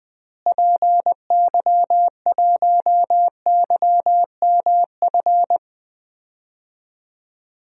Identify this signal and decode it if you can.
Morse code, heard as a keyed tone, about 19 wpm, 700 Hz. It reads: PY1YMF
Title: DXCC-100 Call Signs - 6 Characters - Repeated 2x - 20wpm